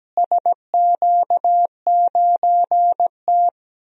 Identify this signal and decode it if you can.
Morse code, heard as a keyed tone, about 17 wpm, 700 Hz. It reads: SQ9T
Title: DXCC-100 Call Signs - 4 Characters - Repeated 3x - 17wpm